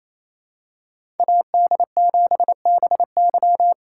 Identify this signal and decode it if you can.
Morse code, heard as a keyed tone, about 28 wpm, 700 Hz. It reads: AD7BY
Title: USA Call Signs - 5 Characters - 6x Character Spacing - 28wpm